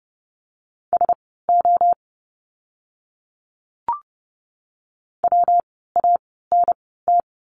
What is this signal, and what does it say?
Morse code, heard as a keyed tone, about 30 wpm, spaced as for 17 wpm, 700 Hz. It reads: SO WANT
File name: course / 30:10wpm / 230 Lesson 37f - Letter x - Words 30wpm